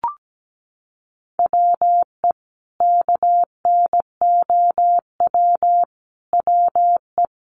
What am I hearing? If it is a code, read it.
Morse code, heard as a keyed tone, about 17 wpm, 700 Hz. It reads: WE KNOW WE
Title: Sentences from Top 1100 Words 17wpm